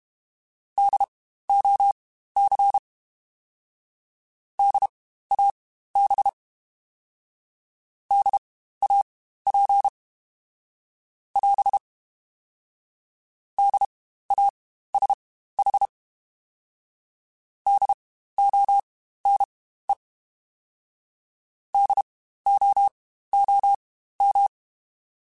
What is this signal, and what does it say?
Morse code, heard as a keyed tone, about 32 wpm, spaced as for 15 wpm, 800 Hz. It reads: DOC DAB DAP <AS> DASH DONE DOOM
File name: Spaced Alphas / Spaced Alphas mp3s / Ds sp @30